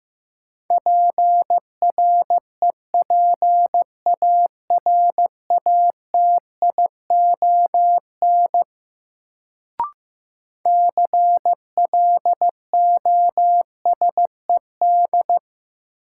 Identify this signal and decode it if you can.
Morse code, heard as a keyed tone, about 15 wpm, 700 Hz. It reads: PREPARATION CLOSED
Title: Sentences from Top 1400 Words - New Word Drill Alt - 15wpm